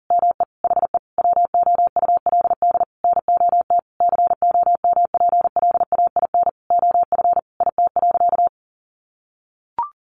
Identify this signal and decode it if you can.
Morse code, heard as a keyed tone, about 40 wpm, 700 Hz. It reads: ME HE WOULD NOT COMPLAIN OF IT.
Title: Sentences from Top 1300 Words 40wpm